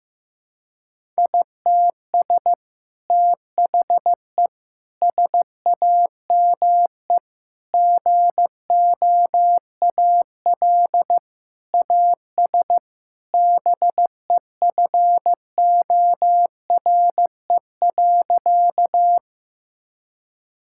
Morse code, heard as a keyed tone, about 15 wpm, 700 Hz. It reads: ITS THE SAME GOAL AS BEFORE.
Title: Sentences from Top 1000 Words - Review Part 5 - 15wpm